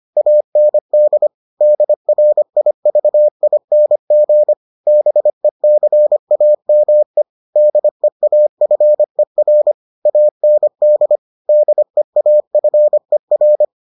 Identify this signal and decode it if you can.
Morse code, heard as a keyed tone, about 25 wpm, 600 Hz. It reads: AND DRIVING BECAME DEAFER AND DEAFER